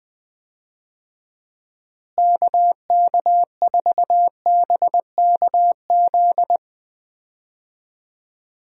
Morse code, heard as a keyed tone, about 20 wpm, 700 Hz. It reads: KK4BKZ
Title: USA Call Signs - 6 Characters - Repeated 3x - 20wpm